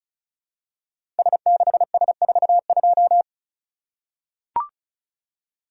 Morse code, heard as a keyed tone, about 36 wpm, 700 Hz. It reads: S6S42
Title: Code Groups - 5 Letter:Numbers - 35wpm